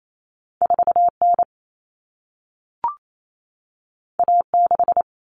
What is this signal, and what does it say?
Morse code, heard as a keyed tone, about 28 wpm, 700 Hz. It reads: 4N A6
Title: Common Cut Numbers - 2-Digit 28wpm